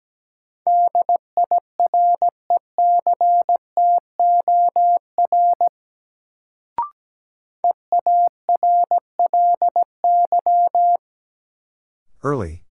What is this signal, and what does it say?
Morse code, heard as a keyed tone, about 17 wpm, 700 Hz. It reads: DIRECTOR EARLY
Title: Sets of 3 Words - New Word Drill - 17wpm